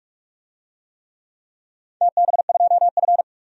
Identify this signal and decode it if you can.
Morse code, heard as a keyed tone, about 45 wpm, 700 Hz. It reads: TB2F
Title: DXCC-100 Call Signs - 4 Characters - 45wpm